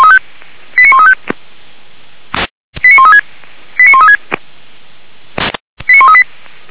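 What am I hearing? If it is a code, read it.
Five-tone selective call (ZVEI), heard as quick runs of five tones, tones of about 70 ms. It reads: 89135 89136 89136 89138
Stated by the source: ZVEI five-tone call 89134 received by SDR, from the Feuerwehr Orlamünde audio server.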